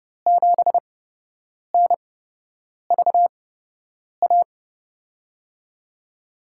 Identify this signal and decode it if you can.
Morse code, heard as a keyed tone, about 30 wpm, spaced as for 8 wpm, 700 Hz. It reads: 7NVA
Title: USA Call Signs - 8x Character Spacing - 30wpm